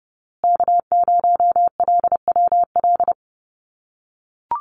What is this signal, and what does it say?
Morse code, heard as a keyed tone, about 30 wpm, 700 Hz. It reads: K0LWL
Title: USA Call Signs Repeated 3x 30wpm